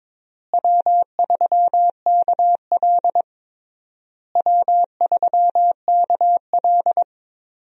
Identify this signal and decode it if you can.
Morse code, heard as a keyed tone, about 22 wpm, 700 Hz. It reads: W3KL W3KL
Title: USA Call Signs - 4 Characters - Repeated 3x - 22wpm